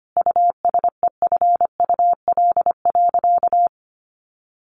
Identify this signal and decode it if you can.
Morse code, heard as a keyed tone, about 25 wpm, 700 Hz. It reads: USEFUL.
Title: Sentences from Top 700 Words 25wpm